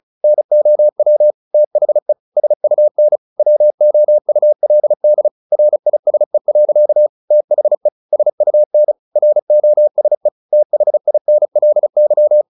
Morse code, heard as a keyed tone, about 35 wpm, 600 Hz. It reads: NOW THE SUN WOULD RISE. THE SUN ROSE THINLY
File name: the old man and the sea / the old man and the sea 1 35wpm